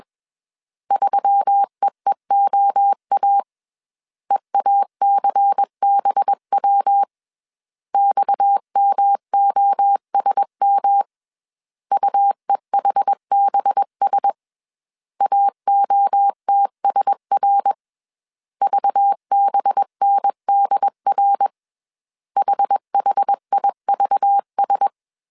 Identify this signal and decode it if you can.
Morse code, heard as a keyed tone, about 21 wpm, 800 Hz. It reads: 3EEOA EACBW XMOSM UEHBS AOTSR VBNDR HHIVS
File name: CW / DET / 1 6 1500C S A 20wpm 2048